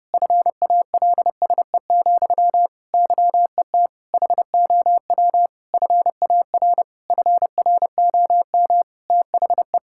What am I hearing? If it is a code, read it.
Morse code, heard as a keyed tone, about 30 wpm, 700 Hz. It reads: FALSE, YET HOW FAR FROM THE
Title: Sentences from Top 1600 Words 30wpm